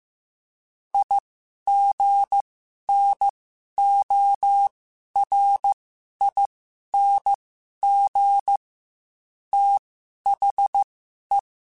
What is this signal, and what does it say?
Morse code, heard as a keyed tone, about 15 wpm, spaced as for 11 wpm, 800 Hz. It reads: IGNORING THE
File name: Quotes / Quotes-mp3s / IgnoringFacts @13